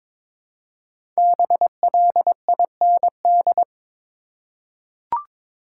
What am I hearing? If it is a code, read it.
Morse code, heard as a keyed tone, about 22 wpm, 700 Hz. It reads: BLIND
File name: 5 Letter Words - First 3 Letters Given - 22wpm